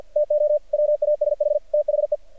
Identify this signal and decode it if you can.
Morse code, heard as a keyed tone, about 50 wpm, 600 Hz. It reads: TO WASH THE
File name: archived / CRSnet